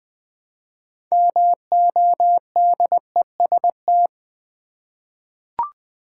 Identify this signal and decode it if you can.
Morse code, heard as a keyed tone, about 20 wpm, 700 Hz. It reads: MODEST
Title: 6 Letter Words - First 3 Letters Given - 20wpm